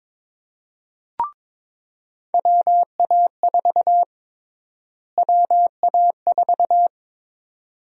Morse code, heard as a keyed tone, about 22 wpm, 700 Hz. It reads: WA4 WA4
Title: USA Call Sign Prefixes - Repeated 2x - 22wpm